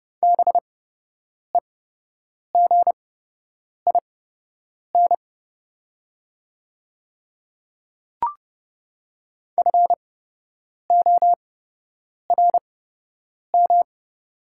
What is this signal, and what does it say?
Morse code, heard as a keyed tone, about 29 wpm, spaced as for 8 wpm, 700 Hz. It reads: BEGIN FORM
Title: 8x Character Spacing - Top 300 Words - 30wpm